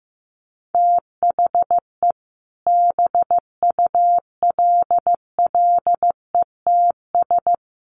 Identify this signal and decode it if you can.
Morse code, heard as a keyed tone, about 14 wpm, 700 Hz. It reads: THE BULLETS
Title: A Sound Of Thunder 15wpm